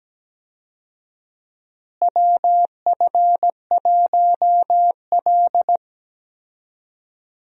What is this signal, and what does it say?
Morse code, heard as a keyed tone, about 17 wpm, 700 Hz. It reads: WF1L
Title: USA Call Signs - 4 Characters - Repeated 3x - 17wpm